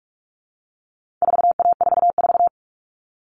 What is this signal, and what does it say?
Morse code, heard as a keyed tone, about 44 wpm, 700 Hz. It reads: 4A44
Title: All Cut Numbers - 4-Digit 45wpm